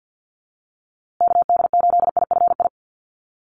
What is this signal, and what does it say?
Morse code, heard as a keyed tone, about 50 wpm, 700 Hz. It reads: KD8IFI
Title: USA Call Signs Repeated 2x 50wpm